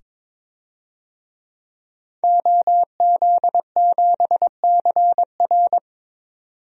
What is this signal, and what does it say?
Morse code, heard as a keyed tone, about 22 wpm, 700 Hz. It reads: OZ7CR
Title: DXCC-100 Call Signs - 5 Characters - 22wpm